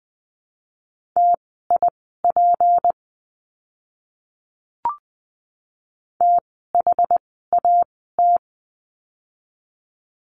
Morse code, heard as a keyed tone, about 20 wpm, spaced as for 15 wpm, 700 Hz. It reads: TIP THAT
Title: 153 Lesson 25f - Letter p - Words 20wpm